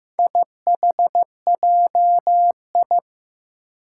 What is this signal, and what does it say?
Morse code, heard as a keyed tone, about 15 wpm, 700 Hz. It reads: IHJI
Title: Four Challenging Letter-Number Combinations 15wpm